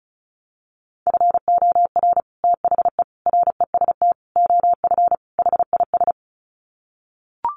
Morse code, heard as a keyed tone, about 34 wpm, 700 Hz. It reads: FOR THE REST OF HIS
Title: Sets of 5 Words 35wpm